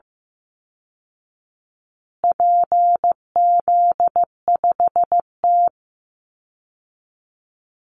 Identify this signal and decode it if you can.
Morse code, heard as a keyed tone, about 15 wpm, 700 Hz. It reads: PZ5T
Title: Call Signs Repeated 2x - 15wpm